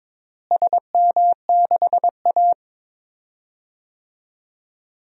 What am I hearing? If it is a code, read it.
Morse code, heard as a keyed tone, about 22 wpm, 700 Hz. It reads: SM6A
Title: DXCC-100 Call Signs - 4 Characters - Repeated 2x - 22wpm